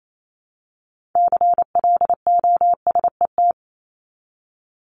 Morse code, heard as a keyed tone, about 28 wpm, 700 Hz. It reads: CLOSET
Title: Top 1751-2000 Words 28wpm